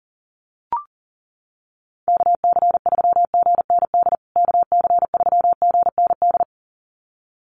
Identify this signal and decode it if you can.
Morse code, heard as a keyed tone, about 40 wpm, 700 Hz. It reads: KC3GND KC3GND
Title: USA Call Signs Repeated 2x 40wpm